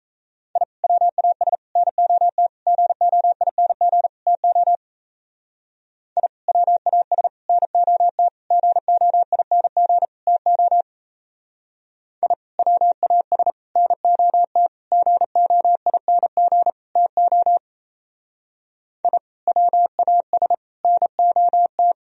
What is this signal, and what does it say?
Morse code, heard as a keyed tone, about 34 wpm, 700 Hz. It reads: I WAS NOT GOING TO I WAS NOT GOING TO I WAS NOT GOING TO I WAS NOT
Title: Speed-Racing - Sets of 5 Words 28wpm